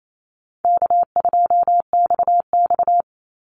Morse code, heard as a keyed tone, about 28 wpm, 700 Hz. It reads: K2XX
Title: Easy Call Signs - 28wpm